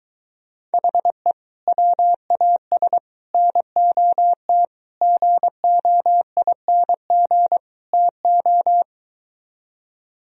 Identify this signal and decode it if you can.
Morse code, heard as a keyed tone, about 23 wpm, 700 Hz. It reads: HE WAS NOT GOING TO